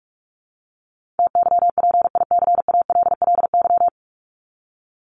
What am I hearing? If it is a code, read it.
Morse code, heard as a keyed tone, about 44 wpm, 700 Hz. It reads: TYPICALLY
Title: LY Suffix - Top 100 Words with Base Words 45wpm